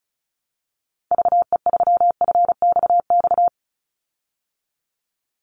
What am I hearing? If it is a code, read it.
Morse code, heard as a keyed tone, about 34 wpm, 700 Hz. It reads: VE3FXX
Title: Call Signs Repeated 2x - 35wpm